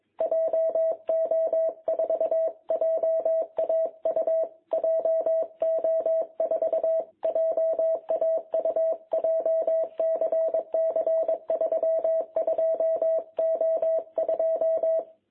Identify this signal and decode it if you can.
Morse code, heard as a keyed tone, about 21 wpm, 660 Hz. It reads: JO4JAU JO4JAU JCC 32O2
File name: jh8cbh / 2021cw